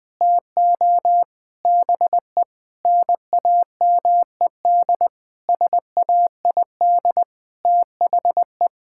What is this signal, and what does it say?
Morse code, heard as a keyed tone, about 20 wpm, 700 Hz. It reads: TO BE NAMED SAID THE
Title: Sentences with 8-Word Phrases - Spoken Beginning - 20wpm